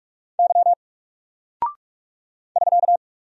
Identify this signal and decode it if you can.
Morse code, heard as a keyed tone, about 44 wpm, 700 Hz. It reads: Y <SK>
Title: Single Character (Letter:Number:Punctuation:Prosign) 45wpm